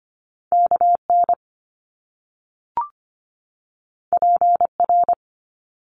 Morse code, heard as a keyed tone, about 25 wpm, 700 Hz. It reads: KN PR
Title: Top Two Letter Combinations in Words 25wpm